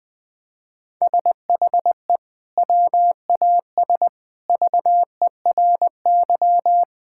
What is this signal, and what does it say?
Morse code, heard as a keyed tone, about 20 wpm, 700 Hz. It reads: SHE WAS VERY
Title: Sentences from Top 1600 Words 20wpm